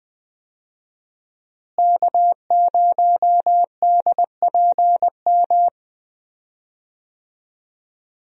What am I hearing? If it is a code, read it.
Morse code, heard as a keyed tone, about 20 wpm, 700 Hz. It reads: K0DPM
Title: USA Call Signs - 5 Characters - Repeated 2x - 20wpm